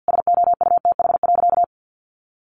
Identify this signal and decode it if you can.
Morse code, heard as a keyed tone, about 50 wpm, 700 Hz. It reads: SOUTH.
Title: Sentences from Top 400 Words 50wpm